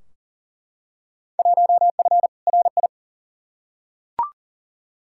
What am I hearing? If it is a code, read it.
Morse code, heard as a keyed tone, about 37 wpm, 700 Hz. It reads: 1F RI
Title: Field Day Exchanges Encore - 40wpm